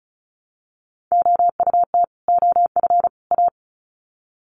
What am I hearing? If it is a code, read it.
Morse code, heard as a keyed tone, about 34 wpm, 700 Hz. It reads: OUT OF A
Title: Sets of 3 Words 35wpm